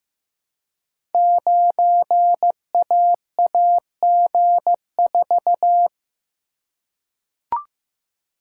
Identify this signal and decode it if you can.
Morse code, heard as a keyed tone, about 15 wpm, 700 Hz. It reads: 9AAG4
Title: Code Groups - 5 Letter:Numbers - 15wpm